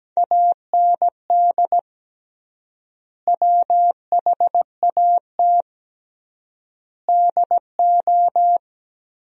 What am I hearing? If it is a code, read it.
Morse code, heard as a keyed tone, about 17 wpm, 700 Hz. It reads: AND WHAT DO
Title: 3x Word-Spacing - Sentences from Top 200 Words - 17wpm